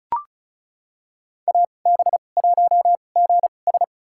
Morse code, heard as a keyed tone, about 36 wpm, spaced as for 25 wpm, 700 Hz. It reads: AB1GS
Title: USA Call Signs - 2x Character Spacing - 35wpm